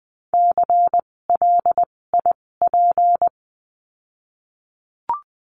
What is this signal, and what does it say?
Morse code, heard as a keyed tone, about 20 wpm, spaced as for 16 wpm, 700 Hz. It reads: CLIP
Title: 256 Lesson 42a - Words 12-20wpm